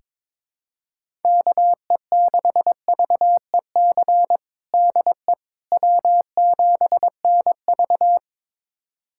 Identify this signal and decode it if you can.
Morse code, heard as a keyed tone, about 22 wpm, 700 Hz. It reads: KE6VEC DE W7NV
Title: USA Callsign DE Callsign 22wpm